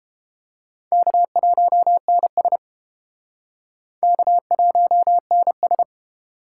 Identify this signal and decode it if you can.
Morse code, heard as a keyed tone, about 33 wpm, 700 Hz. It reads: K1NS K1NS